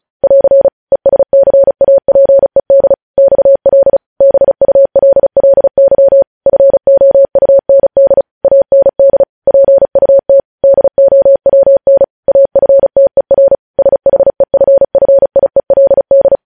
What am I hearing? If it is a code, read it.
Morse code, heard as a keyed tone, about 36 wpm, 550 Hz. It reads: <AR> ESCAPED XL BULLY FOUND AND PUT DOWN AFTER SHEFFIELD
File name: podcast / News headlines-1x-35x35